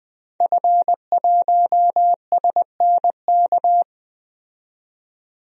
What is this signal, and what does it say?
Morse code, heard as a keyed tone, about 20 wpm, 700 Hz. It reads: F1SNK
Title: DXCC-100 Call Signs - 5 Characters - 20wpm